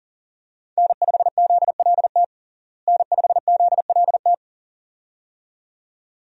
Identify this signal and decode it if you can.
Morse code, heard as a keyed tone, about 37 wpm, 700 Hz. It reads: N5ZLT N5ZLT
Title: USA Call Signs - 5 Characters - Repeated 2x - 40wpm